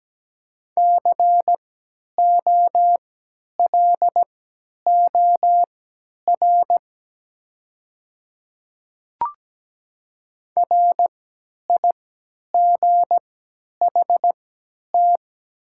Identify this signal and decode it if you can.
Morse code, heard as a keyed tone, about 17 wpm, spaced as for 10 wpm, 700 Hz. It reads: COLOR RIGHT
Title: 3x Character Spacing - Top 300 Words - 17wpm